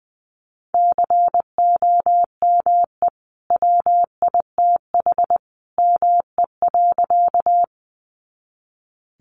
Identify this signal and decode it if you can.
Morse code, heard as a keyed tone, about 20 wpm, 700 Hz. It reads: COME WITH ME.
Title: Sentences from Top 200 Words 20wpm